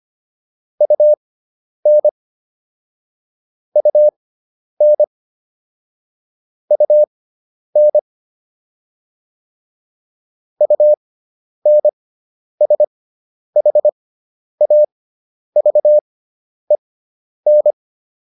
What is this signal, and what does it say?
Morse code, heard as a keyed tone, about 25 wpm, spaced as for 10 wpm, 600 Hz. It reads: UN UN UN UNSHAVEN